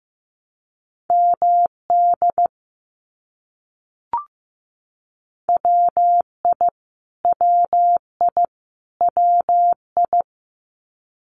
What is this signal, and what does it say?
Morse code, heard as a keyed tone, about 15 wpm, 700 Hz. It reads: MD WI WI WI
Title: US States Repeated 3x 15wpm